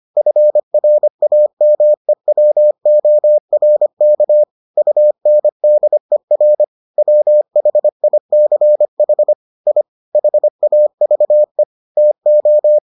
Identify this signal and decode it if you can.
Morse code, heard as a keyed tone, about 25 wpm, 600 Hz. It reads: FRAMEWORK UNDER WHICH I HAVE TO